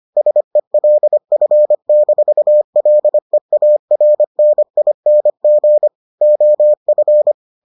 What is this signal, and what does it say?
Morse code, heard as a keyed tone, about 25 wpm, 600 Hz. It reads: SELF-LEARNING OF